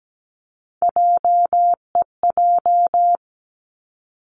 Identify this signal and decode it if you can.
Morse code, heard as a keyed tone, about 17 wpm, 700 Hz. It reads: JEJ
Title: Three Challenging Letter-Number Combinations 17wpm